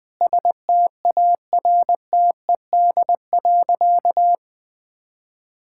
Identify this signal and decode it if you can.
Morse code, heard as a keyed tone, about 21 wpm, 700 Hz. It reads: STARTED.
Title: Sentences with 5-Word Phrases - 20wpm